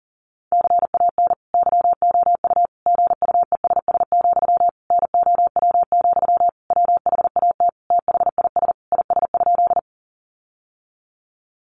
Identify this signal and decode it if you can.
Morse code, heard as a keyed tone, about 40 wpm, 700 Hz. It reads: CAN YOU GUESS, NOW, WHAT THIS IS?
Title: Sentences from Top 1500 Words 40wpm